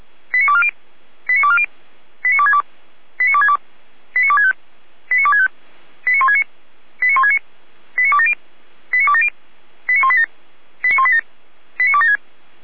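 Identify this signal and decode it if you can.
Five-tone selective call (ZVEI), heard as quick runs of five tones, tones of about 70 ms. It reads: 89240 89240 89262 89262 89265 89265 89169 89169 89270 89270 89187 89187 89276
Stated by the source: ZVEI five-tone selective call received on SDR.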